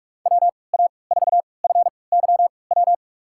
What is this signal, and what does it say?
Morse code, heard as a keyed tone, about 43 wpm, spaced as for 25 wpm, 700 Hz. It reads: WA4FYW
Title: USA Call Signs - 3x Character Spacing - 45wpm